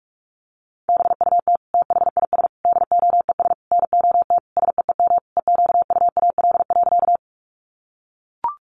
Morse code, heard as a keyed tone, about 44 wpm, 700 Hz. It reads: BUT THIS DOES NOT SEEM EQUAL.
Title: Sentences from Top 1400 Words 45wpm